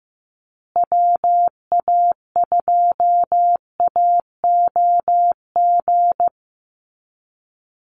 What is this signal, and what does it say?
Morse code, heard as a keyed tone, about 15 wpm, 700 Hz. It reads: WA2AOG
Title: Call Signs Repeated 2x - 15wpm